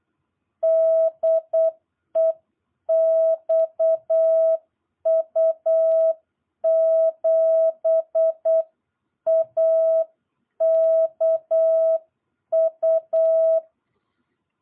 Morse code, heard as a keyed tone, about 8 wpm, 660 Hz. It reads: DE XU7AKU
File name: jh8cbh / 2022cw